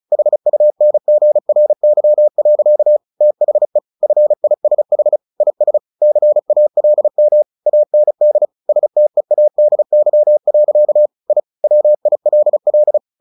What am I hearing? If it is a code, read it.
Morse code, heard as a keyed tone, about 34 wpm, 600 Hz. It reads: HUNGRY. THE FISH IS CALM AND STEADY. I WILL
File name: the old man and the sea / the old man and the sea 1 35wpm